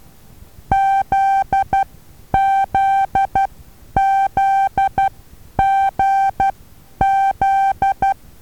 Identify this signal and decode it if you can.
Morse code, heard as a keyed tone, about 12 wpm, spaced as for 9 wpm, 780 Hz. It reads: ZZZGZ
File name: Taviro HA6VA B